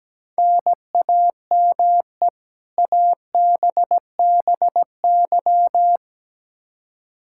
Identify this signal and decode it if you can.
Morse code, heard as a keyed tone, about 17 wpm, 700 Hz. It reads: NAME ABBY
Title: Practice QSO - Exchange 1 - Beginning - 17wpm